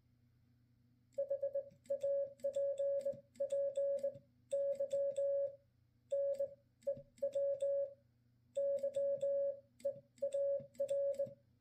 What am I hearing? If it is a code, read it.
Morse code, heard as a keyed tone, about 20 wpm, spaced as for 15 wpm, 560 Hz. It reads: HAPPY NEW YEAR